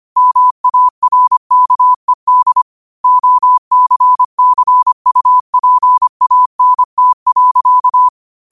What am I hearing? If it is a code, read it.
Morse code, heard as a keyed tone, about 25 wpm, 1000 Hz. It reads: MARKED OCCUPANT.